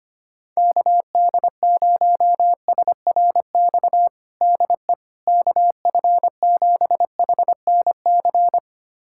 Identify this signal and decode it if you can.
Morse code, heard as a keyed tone, about 25 wpm, 700 Hz. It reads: KD0SRX DE KF7HNC
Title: USA Callsign DE Callsign - 6 Characters - 25wpm